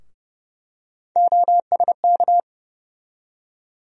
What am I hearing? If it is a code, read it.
Morse code, heard as a keyed tone, about 30 wpm, 700 Hz. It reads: OSK
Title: USA Call Sign Postfixes - 3 Letters 30wpm